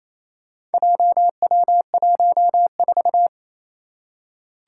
Morse code, heard as a keyed tone, about 27 wpm, 700 Hz. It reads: JW14
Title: Four Challenging Letter-Number Combinations 28wpm